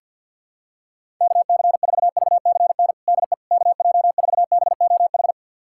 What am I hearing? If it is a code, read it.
Morse code, heard as a keyed tone, about 47 wpm, 690 Hz. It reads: KC4VCN DE KJ4BOH
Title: USA Callsign DE Callsign - 6 Characters - 50wpm